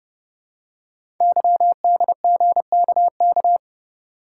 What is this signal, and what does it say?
Morse code, heard as a keyed tone, about 30 wpm, 700 Hz. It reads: YDGKK
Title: Code Groups - 5 Letters - 30wpm